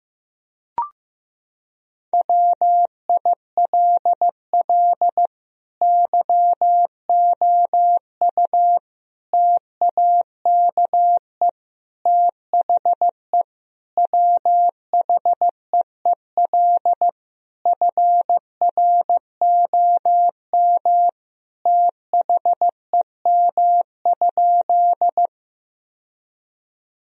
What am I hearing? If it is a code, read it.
Morse code, heard as a keyed tone, about 14 wpm, 700 Hz. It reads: WILL YOU TAKE THE WHEEL FROM THEM?
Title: Sentences from Top 1000 Words - Review Part 3 - 15wpm